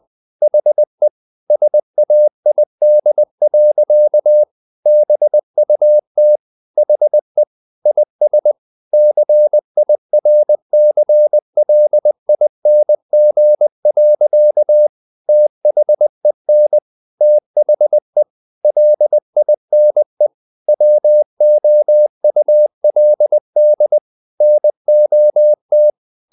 Morse code, heard as a keyed tone, about 20 wpm, 600 Hz. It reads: HE SAID. BUT HE IS CIRCLING. THEN THE LINE WOULD NOT
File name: the old man and the sea part 2 20wpm